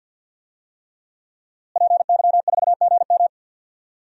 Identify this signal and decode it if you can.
Morse code, heard as a keyed tone, about 48 wpm, 700 Hz. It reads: PY4GM
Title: DXCC-100 Call Signs - 5 Characters - Repeated 3x - 50wpm